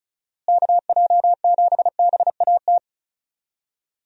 Morse code, heard as a keyed tone, about 34 wpm, 700 Hz. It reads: KJ7BAT
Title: USA Call Signs - 6 Characters - Repeated 3x - 35wpm